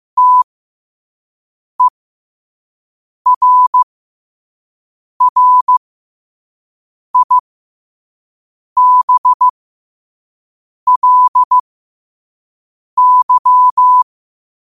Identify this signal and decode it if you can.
Morse code, heard as a keyed tone, about 15 wpm, spaced as for 5 wpm, 1000 Hz. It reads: TERRIBLY
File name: qotdfiles / 5wpm